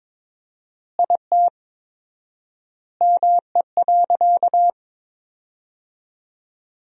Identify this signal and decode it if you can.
Morse code, heard as a keyed tone, about 22 wpm, 700 Hz. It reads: IT ME.
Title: 4x Word-Spacing - Sentences from Top 200 Words - 22wpm